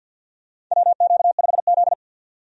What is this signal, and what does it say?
Morse code, heard as a keyed tone, about 48 wpm, 700 Hz. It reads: WQ5Z
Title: USA Call Signs - 4 Characters - 50wpm